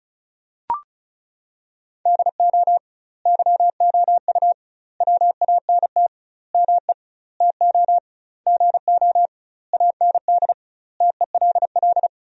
Morse code, heard as a keyed tone, about 36 wpm, 700 Hz. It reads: DO YOU WANT ME TO GO AND TELL
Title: 2x Word-Spacing - Sentences from Top 200 Words - 35wpm